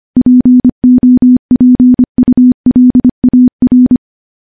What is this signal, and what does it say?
Morse code, heard as a keyed tone, about 25 wpm, 250 Hz. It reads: POPULAR
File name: morse / morse-training-25WPM